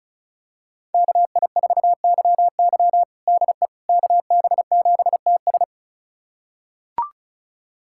Morse code, heard as a keyed tone, about 36 wpm, 700 Hz. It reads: KI4YY DE KB7TS
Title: USA Callsign DE Callsign - 5 Characters - 35wpm